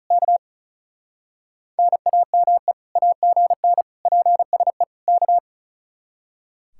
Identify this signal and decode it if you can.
Morse code, heard as a keyed tone, about 45 wpm, 700 Hz. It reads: K NAME AGN PSE K